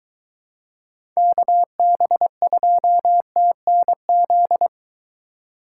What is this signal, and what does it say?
Morse code, heard as a keyed tone, about 23 wpm, 700 Hz. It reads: KB2TNZ